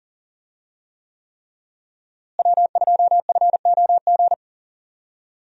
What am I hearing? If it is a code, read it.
Morse code, heard as a keyed tone, about 37 wpm, 700 Hz. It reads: W2FOG
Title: USA Call Signs - 5 Characters - Repeated 3x - 40wpm